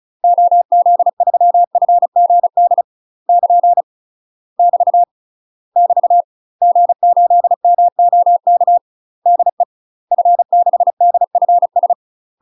Morse code, heard as a keyed tone, about 36 wpm, 700 Hz. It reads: OZ3FGD <KN> <BT> <BT> G8MOK DE F6DFS
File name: UK MASTER QSOs TRAINING / 35WPM / UK MASTER QSOs 0012